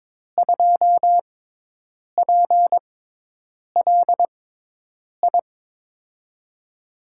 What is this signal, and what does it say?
Morse code, heard as a keyed tone, about 22 wpm, spaced as for 8 wpm, 700 Hz. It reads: 2PLI
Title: USA Call Signs - 5 Characters - 6x Character Spacing - 22wpm